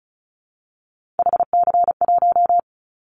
Morse code, heard as a keyed tone, about 34 wpm, 700 Hz. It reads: HC1
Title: Callsign Prefixes for Top 100 DXCC - 35wpm